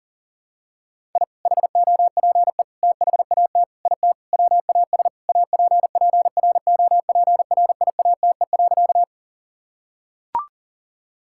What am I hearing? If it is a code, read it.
Morse code, heard as a keyed tone, about 40 wpm, 700 Hz. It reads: I HOPE THAT IT WAS APPROPRIATE.
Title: Sentences from Top 1000 Words - Review Part 5 - 40wpm